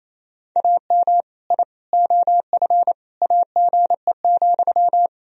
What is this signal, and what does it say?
Morse code, heard as a keyed tone, about 28 wpm, 700 Hz. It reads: AM I OF AGE,
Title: Sentences from Top 1000 Words - Review Part 3 - 28wpm